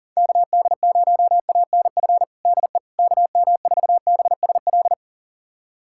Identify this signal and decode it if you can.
Morse code, heard as a keyed tone, about 40 wpm, 700 Hz. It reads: KD0ANF DE KM4BSL
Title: USA Callsign DE Callsign - 6 Characters - 40wpm